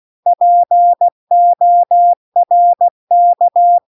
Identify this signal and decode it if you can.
Morse code, heard as a keyed tone, about 16 wpm, 700 Hz. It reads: PORK